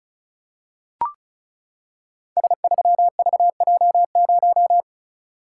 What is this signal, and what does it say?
Morse code, heard as a keyed tone, about 36 wpm, 700 Hz. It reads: S3VJ0
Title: Code Groups - 5 Letter:Numbers - 35wpm